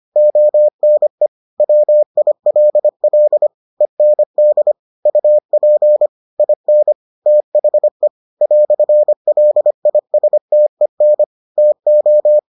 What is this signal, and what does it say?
Morse code, heard as a keyed tone, about 25 wpm, 600 Hz. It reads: ONE WILL END UP IN THE "LISTEN TO